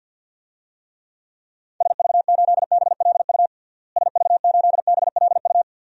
Morse code, heard as a keyed tone, about 51 wpm, 700 Hz. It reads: SV8BLU SV8BLU
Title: DXCC-100 Call Signs - 6 Characters - Repeated 2x - 50wpm